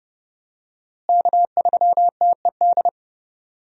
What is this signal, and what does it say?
Morse code, heard as a keyed tone, about 30 wpm, 700 Hz. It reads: K3TED
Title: USA Call Signs - 5 Characters - Repeated 2x - 30wpm